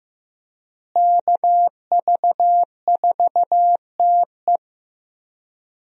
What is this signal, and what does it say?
Morse code, heard as a keyed tone, about 15 wpm, 700 Hz. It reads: KV4TE
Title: USA Call Signs - 5 Characters - Repeated 2x - 15wpm